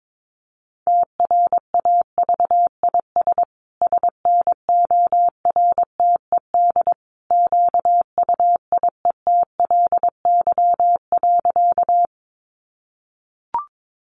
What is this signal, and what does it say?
Morse code, heard as a keyed tone, about 22 wpm, 700 Hz. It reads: TRAVIS SNORTED QUIETLY.
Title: A Sound Of Thunder 22wpm